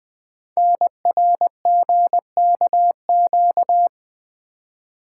Morse code, heard as a keyed tone, about 20 wpm, 700 Hz. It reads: NRGKQ
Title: Code Groups - 5 Letters - 20wpm